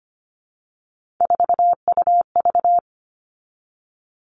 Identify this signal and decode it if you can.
Morse code, heard as a keyed tone, about 25 wpm, 700 Hz. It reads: 4UV
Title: Three Challenging Letter-Number Combinations 25wpm